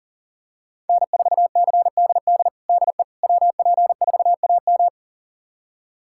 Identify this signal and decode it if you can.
Morse code, heard as a keyed tone, about 40 wpm, 700 Hz. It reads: N4CDD DE WP4AM
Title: USA Callsign DE Callsign - 5 Characters - 40wpm